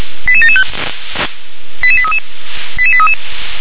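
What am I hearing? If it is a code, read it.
Five-tone selective call (ZVEI), heard as quick runs of five tones, tones of about 70 ms. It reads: 88775 88833 88833
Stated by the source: ZVEI five-tone selective call, code 88819, received on an SDR.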